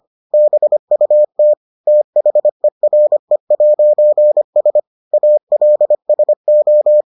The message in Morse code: BUT THERE'S ALSO

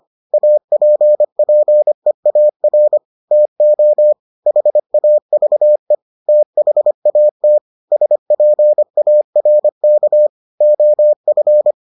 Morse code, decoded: APPEAR TO HAVE THAT SPARK OF

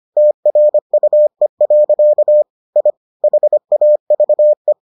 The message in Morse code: TRUE. I HAVE

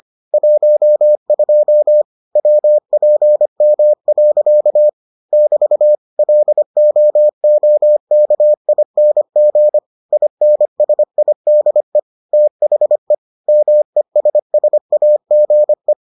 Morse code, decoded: 12 WPM. <BT> LOOKING INSIDE THE MESSAGE

